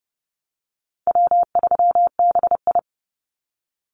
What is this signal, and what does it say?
Morse code, heard as a keyed tone, about 30 wpm, 700 Hz. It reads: W3BI
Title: Easy Call Signs Repeated 3x - 30wpm